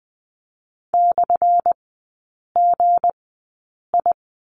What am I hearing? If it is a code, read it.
Morse code, heard as a keyed tone, about 20 wpm, spaced as for 8 wpm, 700 Hz. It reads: /GI
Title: 178 Lesson 29e - Slash - Three Characters 20wpm